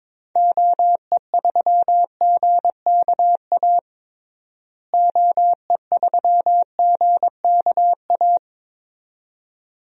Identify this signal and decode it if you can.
Morse code, heard as a keyed tone, about 22 wpm, 700 Hz. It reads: OE3GKA OE3GKA
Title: DXCC-100 Call Signs - 6 Characters - Repeated 2x - 22wpm